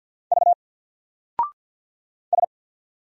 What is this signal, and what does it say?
Morse code, heard as a keyed tone, about 50 wpm, 700 Hz. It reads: V S
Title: Single Letter-Number Alt 50wpm